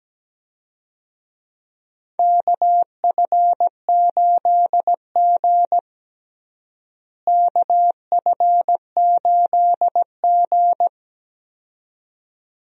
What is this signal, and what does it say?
Morse code, heard as a keyed tone, about 17 wpm, 700 Hz. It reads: KF8G KF8G
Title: USA Call Signs - 4 Characters - Repeated 2x - 17wpm